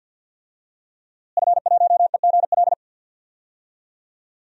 Morse code, heard as a keyed tone, about 50 wpm, 700 Hz. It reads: F1EGL
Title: DXCC-100 Call Signs - 5 Characters - Repeated 2x - 50wpm